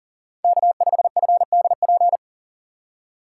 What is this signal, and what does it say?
Morse code, heard as a keyed tone, about 40 wpm, 700 Hz. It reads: K5FDP
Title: USA Call Signs - 6x Character Spacing - 40wpm